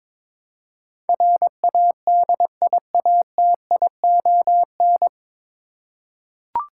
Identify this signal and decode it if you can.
Morse code, heard as a keyed tone, about 21 wpm, 700 Hz. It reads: RADIATION
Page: Morse Code Ninja practice 22wpm